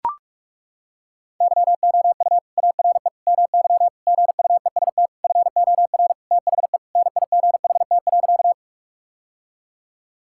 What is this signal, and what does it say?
Morse code, heard as a keyed tone, about 42 wpm, 700 Hz. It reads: YOU ARE MY GUEST FOR THE NIGHT.
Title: Sentences from Top 1600 Words 45wpm